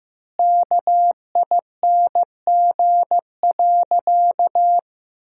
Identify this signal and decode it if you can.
Morse code, heard as a keyed tone, about 15 wpm, 700 Hz. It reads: KING.
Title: 2x Word-Spacing - Sentences from Top 400 Words - 15wpm